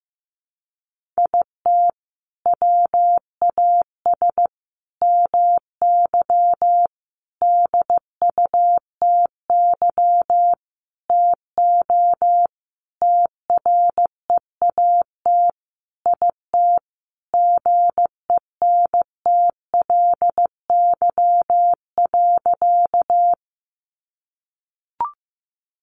Morse code, heard as a keyed tone, about 15 wpm, 700 Hz. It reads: IT WAS MY DUTY TO TREAT IT GENTLY.
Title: Sentences from Top 1500 Words 15wpm